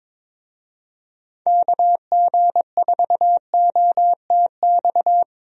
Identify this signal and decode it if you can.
Morse code, heard as a keyed tone, about 21 wpm, 700 Hz. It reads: KG4OTX
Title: USA Call Signs - 6 Characters - Repeated 3x - 22wpm